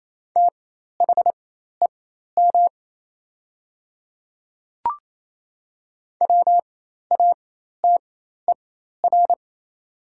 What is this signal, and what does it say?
Morse code, heard as a keyed tone, about 27 wpm, spaced as for 13 wpm, 700 Hz. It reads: THEM WATER
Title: 4x Character Spacing - Top 100 Words - 28wpm